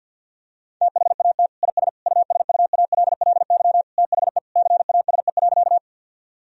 Morse code, heard as a keyed tone, about 50 wpm, 700 Hz. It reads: THAT IS USUALLY THE CASE.
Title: Sentences from Top 1000 Words - Review Part 3 - 50wpm